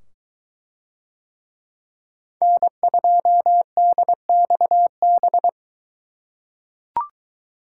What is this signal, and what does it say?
Morse code, heard as a keyed tone, about 22 wpm, 700 Hz. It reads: N2DXB